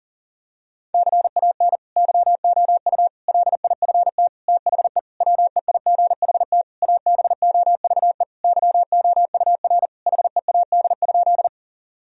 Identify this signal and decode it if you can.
Morse code, heard as a keyed tone, about 40 wpm, 700 Hz. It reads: CAN YOU LIFT THE WEIGHT ABOVE YOUR HEAD?
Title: Sentences from Top 1000 Words - Review Part 3 - 40wpm